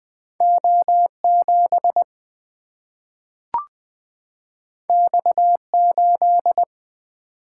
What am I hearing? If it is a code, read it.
Morse code, heard as a keyed tone, about 20 wpm, 700 Hz. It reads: O 7 X 8
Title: Single-Letter Single-Number 20wpm